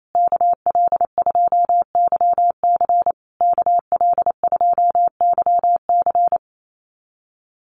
Morse code, heard as a keyed tone, about 28 wpm, 700 Hz. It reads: KL2YC KL2YC
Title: USA Call Signs Repeated 3x 28wpm